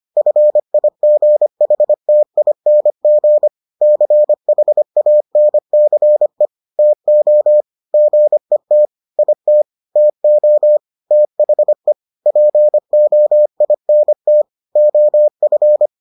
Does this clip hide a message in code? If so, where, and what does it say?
Yes: Morse code from 0.2 to 15.9 s, heard as a keyed tone, about 25 wpm, 600 Hz. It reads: FIGHTING CHANCE TO GET IT TO THE POINT OF